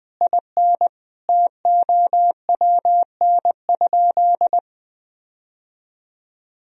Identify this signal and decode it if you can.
Morse code, heard as a keyed tone, about 20 wpm, 700 Hz. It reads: IN TOWN?
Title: Sentences from Top 1000 Words - Review Part 2 - 20wpm